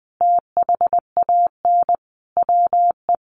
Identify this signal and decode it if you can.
Morse code, heard as a keyed tone, about 20 wpm, 700 Hz. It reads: THAN WE
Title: Sentences from Top 100 Words 20wpm